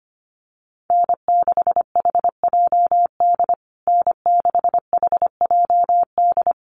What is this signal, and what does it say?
Morse code, heard as a keyed tone, about 25 wpm, 700 Hz. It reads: N6HJD N6HJD
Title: USA Call Signs Repeated 3x 25wpm